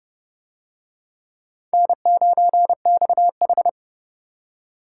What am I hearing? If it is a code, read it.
Morse code, heard as a keyed tone, about 28 wpm, 700 Hz. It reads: N9XH